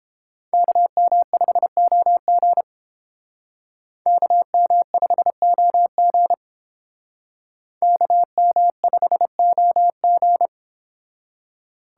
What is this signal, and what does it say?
Morse code, heard as a keyed tone, about 29 wpm, 700 Hz. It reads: KM5OG KM5OG KM5OG